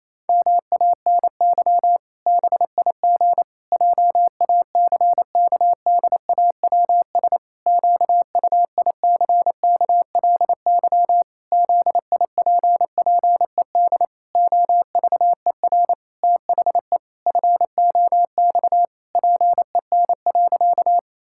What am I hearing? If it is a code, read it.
Morse code, heard as a keyed tone, about 28 wpm, 700 Hz. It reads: MANY BIG JACKDAWS QUICKLY ZIPPED OVER THE FOX PEN.
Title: Pangram Challenge v2 - 28wpm